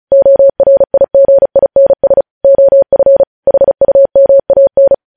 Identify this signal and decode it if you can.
Morse code, heard as a keyed tone, about 36 wpm, 550 Hz. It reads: ORIGINS OF HUMAN
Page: News Headlines in Morse Code at 35 WPM